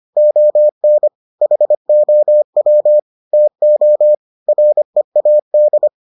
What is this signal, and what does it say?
Morse code, heard as a keyed tone, about 25 wpm, 600 Hz. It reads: ON HOW TO READ